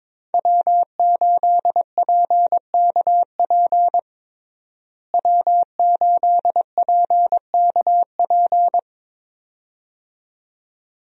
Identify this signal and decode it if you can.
Morse code, heard as a keyed tone, about 22 wpm, 700 Hz. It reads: W8PKP W8PKP
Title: USA Call Signs - 5 Characters - Repeated 2x - 22wpm